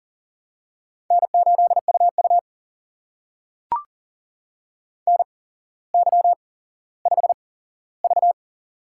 Morse code, heard as a keyed tone, about 41 wpm, spaced as for 17 wpm, 690 Hz. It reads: N8UU NY5V
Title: USA Call Signs - 4 Characters - 8x Character Spacing - 40wpm